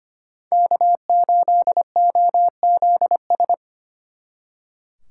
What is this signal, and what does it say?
Morse code, heard as a keyed tone, about 28 wpm, 700 Hz. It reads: K8OZS
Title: Speed Racing - USA Call Signs 25wpm